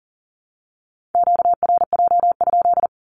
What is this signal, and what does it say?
Morse code, heard as a keyed tone, about 40 wpm, 700 Hz. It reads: QRJ?
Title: Q-Codes Repeated 3x 40wpm